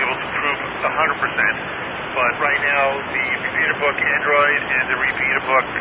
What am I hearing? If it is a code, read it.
Emergency Alert System (SAME) tones, heard as harsh buzzing bursts, with speech. It is heard from 1.2 to 5.4 s.